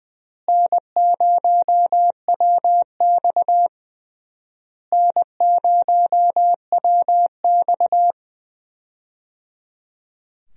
Morse code, heard as a keyed tone, about 20 wpm, 700 Hz. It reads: N0WX N0WX
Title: USA Call Signs - 4 Characters - Repeated 3x - 20wpm